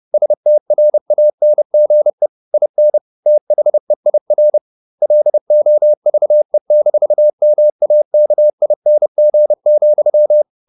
Morse code, heard as a keyed tone, about 30 wpm, 600 Hz. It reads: STRANGE IN THEIR LOVE-MAKING,
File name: the old man and the sea / the old man and the sea 1 30wpm